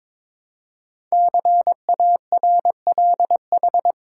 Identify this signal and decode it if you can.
Morse code, heard as a keyed tone, about 22 wpm, 700 Hz. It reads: CARLH
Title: Code Groups - 5 Letters - 22wpm